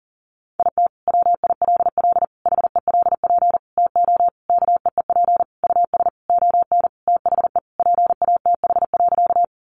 Morse code, heard as a keyed tone, about 37 wpm, 700 Hz. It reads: IT WILL HELP TO KEEP US ON THE PATH.
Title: Sentences from Top 1400 Words 40wpm